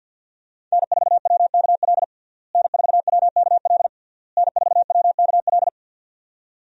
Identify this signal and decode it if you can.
Morse code, heard as a keyed tone, about 46 wpm, 700 Hz. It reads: N4WKL N4WKL N4WKL
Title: USA Call Signs - 5 Characters - Repeated 3x - 50wpm